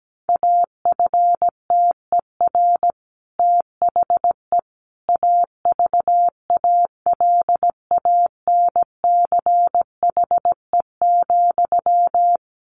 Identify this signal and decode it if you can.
Morse code, heard as a keyed tone, about 17 wpm, 700 Hz. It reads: AFTER THE AVALANCHE,
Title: A Sound Of Thunder 17wpm